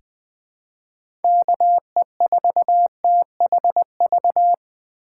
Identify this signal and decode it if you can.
Morse code, heard as a keyed tone, about 21 wpm, 700 Hz. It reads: KE4THV
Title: USA Call Signs - 6 Characters - 3x Character Spacing - 20wpm